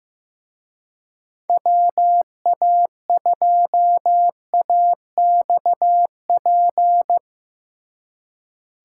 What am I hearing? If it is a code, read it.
Morse code, heard as a keyed tone, about 17 wpm, 700 Hz. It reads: WA2AXP
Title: Speed Racing - USA Call Signs 15wpm